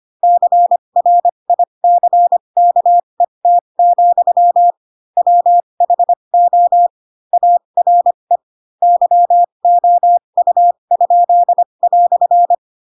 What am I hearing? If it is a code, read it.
Morse code, heard as a keyed tone, about 25 wpm, spaced as for 22 wpm, 700 Hz. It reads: CRICKET, WHO ARE YOU?"
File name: Pinocchio / Pinocchio ch4 18 25 ik1hge